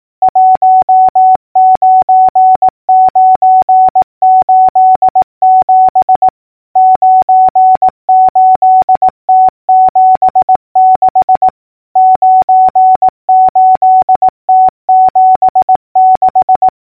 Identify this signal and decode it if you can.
Morse code, heard as a keyed tone, about 18 wpm, 750 Hz. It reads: 19987 98T76 98T76
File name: numbers / V2-morse-clicks